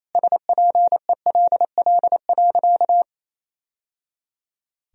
Morse code, heard as a keyed tone, about 28 wpm, 700 Hz. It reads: SPELL.
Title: Sentences from Top 1000 Words - Review Part 1 - 28wpm